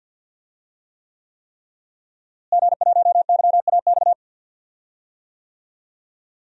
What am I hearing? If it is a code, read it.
Morse code, heard as a keyed tone, about 52 wpm, 700 Hz. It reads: G1YAX
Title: DXCC-100 Call Signs - 5 Characters - Repeated 3x - 50wpm